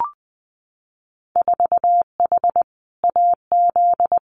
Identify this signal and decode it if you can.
Morse code, heard as a keyed tone, about 20 wpm, 700 Hz. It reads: 4H AZ
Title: Winter Field Day Exchanges Encore 20wpm